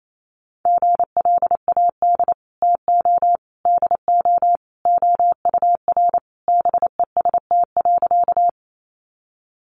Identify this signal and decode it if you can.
Morse code, heard as a keyed tone, about 28 wpm, 700 Hz. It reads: GLAD TO DO OUR BEST.
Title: Sentences from Top 800 Words 28wpm